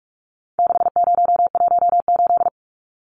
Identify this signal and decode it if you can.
Morse code, heard as a keyed tone, about 44 wpm, 700 Hz. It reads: 6018
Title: Numbers - 4-Digit 45wpm